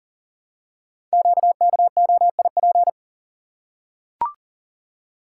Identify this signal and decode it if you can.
Morse code, heard as a keyed tone, about 42 wpm, 700 Hz. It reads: QKOIP
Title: Code Groups - 5 Letters - 40wpm